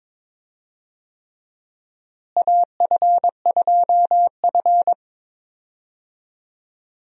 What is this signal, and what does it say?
Morse code, heard as a keyed tone, about 22 wpm, 700 Hz. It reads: AF2F
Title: USA Call Signs - 4 Characters - Repeated 3x - 22wpm